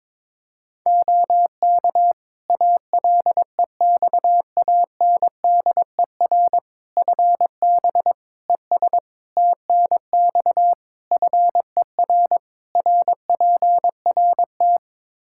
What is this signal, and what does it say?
Morse code, heard as a keyed tone, about 21 wpm, 700 Hz. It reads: OK ALEXANDER FB ES TNX FER RPRT
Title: K1USN SST QSO Practice 22wpm